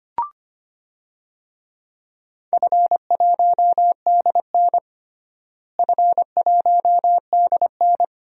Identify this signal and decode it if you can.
Morse code, heard as a keyed tone, about 25 wpm, 700 Hz. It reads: F1DN F1DN
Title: DXCC-100 Call Signs - 4 Characters - Repeated 2x - 25wpm